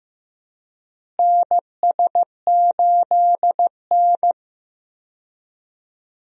Morse code, heard as a keyed tone, about 15 wpm, 700 Hz. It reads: NS8N
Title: USA Call Signs - 4 Characters - Repeated 3x - 15wpm